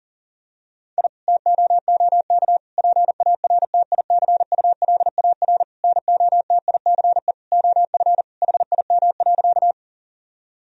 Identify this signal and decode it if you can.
Morse code, heard as a keyed tone, about 40 wpm, 700 Hz. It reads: I TOOK PARTICULAR NOTICE OF HIM.
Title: Sentences from Top 1000 Words - Review Part 4 - 40wpm